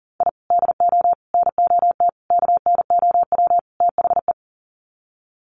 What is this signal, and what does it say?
Morse code, heard as a keyed tone, about 39 wpm, 700 Hz. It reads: I DO NOT KNOW THE
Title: Sets of 5 Words 40wpm